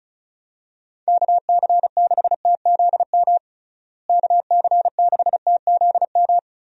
Morse code, heard as a keyed tone, about 36 wpm, 700 Hz. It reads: KC6TZM KC6TZM
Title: USA Call Signs - 6 Characters - Repeated 2x - 35wpm